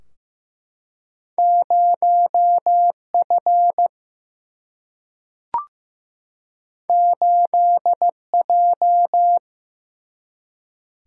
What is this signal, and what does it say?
Morse code, heard as a keyed tone, about 15 wpm, 700 Hz. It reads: 0 F 8 J
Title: Single-Letter Single-Number 15wpm